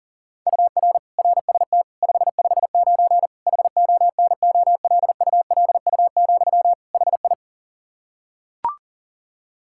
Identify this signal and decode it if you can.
Morse code, heard as a keyed tone, about 37 wpm, 700 Hz. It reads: UR RST 559 HONOLULU, HI
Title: Practice QSO - Exchange 1 - Beginning - 40wpm